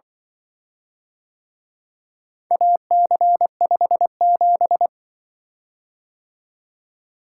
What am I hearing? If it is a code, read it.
Morse code, heard as a keyed tone, about 24 wpm, 700 Hz. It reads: AC57
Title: Speed-Racing - Four Letter-Number Combinations - 20wpm